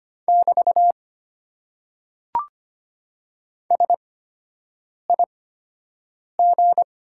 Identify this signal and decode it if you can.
Morse code, heard as a keyed tone, about 25 wpm, spaced as for 7 wpm, 700 Hz. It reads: <BT> SIG
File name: Top 90 QSO Elements - 8x Character Spacing - 25wpm